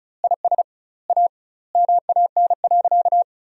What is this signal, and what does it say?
Morse code, heard as a keyed tone, about 34 wpm, 700 Hz. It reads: IS A MAN.
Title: 2x Word-Spacing - Sentences from Top 500 Words - 35wpm